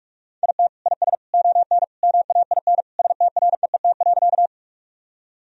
Morse code, heard as a keyed tone, about 46 wpm, 700 Hz. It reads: IT IS ON MAIN STREET.
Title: Sentences from Top 1000 Words - Review Part 2 - 45wpm